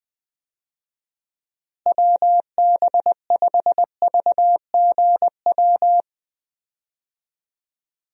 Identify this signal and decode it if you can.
Morse code, heard as a keyed tone, about 20 wpm, 700 Hz. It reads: WB5VGW
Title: USA Call Signs - 6 Characters - Repeated 3x - 20wpm